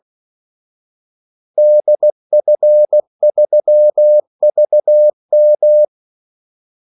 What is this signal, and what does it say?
Morse code, heard as a keyed tone, about 16 wpm, 600 Hz. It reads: DF3VM
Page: Kaldesignaler 16 wpm